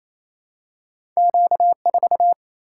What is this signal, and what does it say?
Morse code, heard as a keyed tone, about 28 wpm, 700 Hz. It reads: Q 4
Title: Single-Letter Single-Number 28wpm